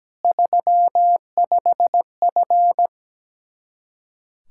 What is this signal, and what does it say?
Morse code, heard as a keyed tone, about 17 wpm, 700 Hz. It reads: 35F
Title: Practice QSO - Exchange 2 - Info - 17wpm